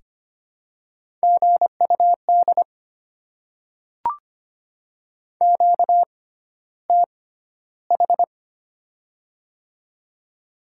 Morse code, heard as a keyed tone, about 25 wpm, spaced as for 13 wpm, 700 Hz. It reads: GUD QTH
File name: Top 20 QSO Elements - 6x Character Spacing - 25wpm